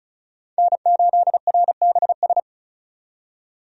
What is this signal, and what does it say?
Morse code, heard as a keyed tone, about 36 wpm, 700 Hz. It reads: N8RBS
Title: USA Call Signs - 5 Characters - 35wpm